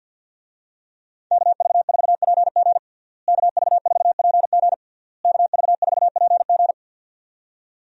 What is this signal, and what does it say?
Morse code, heard as a keyed tone, about 51 wpm, 700 Hz. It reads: KV4PG KV4PG KV4PG
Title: USA Call Signs - 5 Characters - Repeated 3x - 50wpm